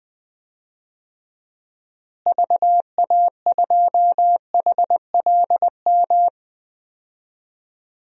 Morse code, heard as a keyed tone, about 20 wpm, 700 Hz. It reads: VA2HLM
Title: DXCC-100 Call Signs - 6 Characters - Repeated 3x - 20wpm